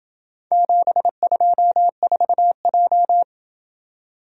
Morse code, heard as a keyed tone, about 26 wpm, 700 Hz. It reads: 724J